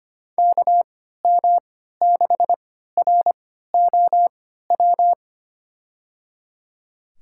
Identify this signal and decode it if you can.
Morse code, heard as a keyed tone, about 25 wpm, spaced as for 14 wpm, 700 Hz. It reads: KM6ROW
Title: USA Call Signs - 6 Characters - 3x Character Spacing - 25wpm